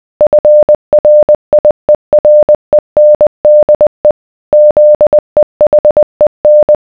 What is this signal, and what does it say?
Morse code, heard as a keyed tone, about 20 wpm, 600 Hz. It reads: FRIERENDE ZEHEN